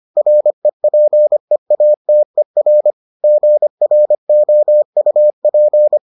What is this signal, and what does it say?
Morse code, heard as a keyed tone, about 25 wpm, 600 Hz. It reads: REPEATER GROUP